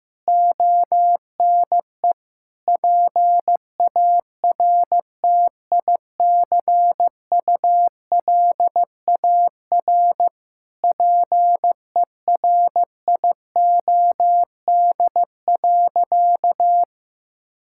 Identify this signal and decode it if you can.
Morse code, heard as a keyed tone, about 15 wpm, 700 Hz. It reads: ONE PARTICULAR PERIOD.
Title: Sentences from Top 1000 Words - Review Part 5 - 15wpm